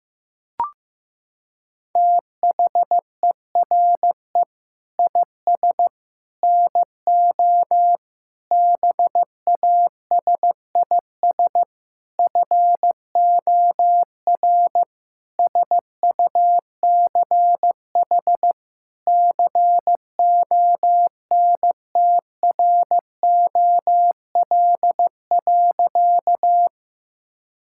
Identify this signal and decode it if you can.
Morse code, heard as a keyed tone, about 15 wpm, 700 Hz. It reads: THERE IS NO BASIS FOR SUCH CONTROL.
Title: Sentences from Top 1000 Words - Review Part 5 - 15wpm